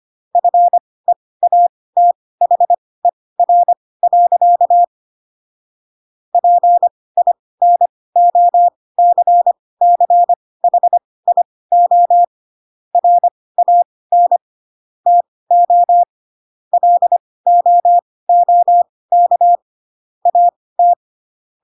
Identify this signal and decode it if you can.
Morse code, heard as a keyed tone, about 25 wpm, spaced as for 18 wpm, 700 Hz. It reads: FEATHER. PINOCCHIO RAN TO LOOK AT